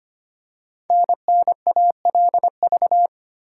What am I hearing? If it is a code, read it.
Morse code, heard as a keyed tone, about 25 wpm, 700 Hz. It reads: NNALV
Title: Code Groups - 5 Letter:Numbers - 25wpm